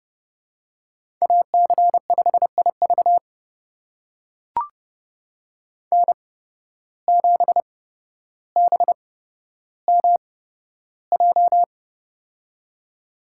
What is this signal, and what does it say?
Morse code, heard as a keyed tone, about 30 wpm, spaced as for 13 wpm, 700 Hz. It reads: AC5IV N7BMJ
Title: USA Call Signs - 5 Characters - 8x Character Spacing - 30wpm